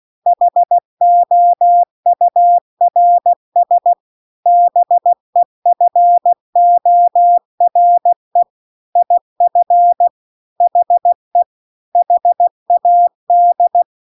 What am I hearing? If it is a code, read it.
Morse code, heard as a keyed tone, about 16 wpm, 700 Hz. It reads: HOURS BEFORE IF HE HAD